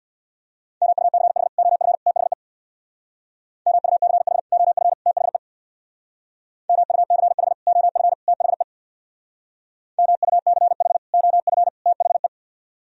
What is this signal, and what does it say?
Morse code, heard as a keyed tone, about 64 wpm, 700 Hz. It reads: MUCH OF THE MUCH OF THE MUCH OF THE MUCH OF THE